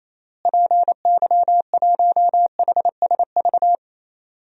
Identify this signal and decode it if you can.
Morse code, heard as a keyed tone, about 28 wpm, 700 Hz. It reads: PY1HSV
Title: DXCC-100 Call Signs - 6 Characters - Repeated 2x - 28wpm